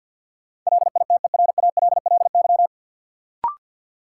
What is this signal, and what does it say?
Morse code, heard as a keyed tone, about 53 wpm, 700 Hz. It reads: LITERALLY
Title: Sentences from Top 1400 Words - New Word Drill Alt - 50wpm